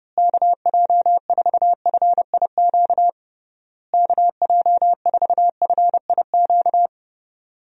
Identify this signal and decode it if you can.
Morse code, heard as a keyed tone, about 30 wpm, 700 Hz. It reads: KJ4FIQ KJ4FIQ
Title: USA Call Signs - 6 Characters - Repeated 3x - 30wpm